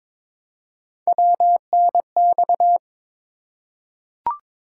Morse code, heard as a keyed tone, about 23 wpm, 700 Hz. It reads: WNX
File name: practice / 22wpm / Three Letter-Number Combinations - 22wpm